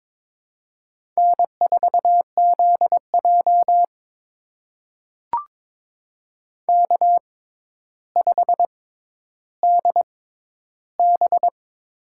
Morse code, heard as a keyed tone, about 22 wpm, spaced as for 11 wpm, 700 Hz. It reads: N4ZJ K5DB
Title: USA Call Signs - 6x Character Spacing - 22wpm